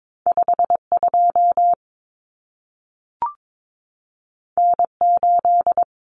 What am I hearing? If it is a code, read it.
Morse code, heard as a keyed tone, about 22 wpm, 700 Hz. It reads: 52 N8
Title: Common Cut Numbers - 2-Digit 22wpm